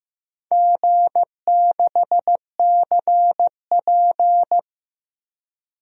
Morse code, heard as a keyed tone, about 15 wpm, 700 Hz. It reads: G6CP
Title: DXCC-100 Call Signs - 4 Characters - 15wpm